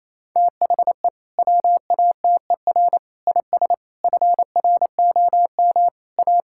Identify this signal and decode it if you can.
Morse code, heard as a keyed tone, about 28 wpm, 700 Hz. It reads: THE WATER IS FROM A
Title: Sentences from Top 1000 Words - Review Part 5 - 28wpm